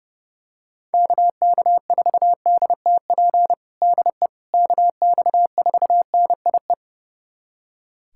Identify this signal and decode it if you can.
Morse code, heard as a keyed tone, about 30 wpm, 700 Hz. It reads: KK4DTP DE KX4NIE
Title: USA Callsign DE Callsign - 6 Characters - 30wpm